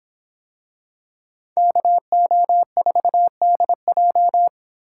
Morse code, heard as a keyed tone, about 25 wpm, 700 Hz. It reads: KO4DJ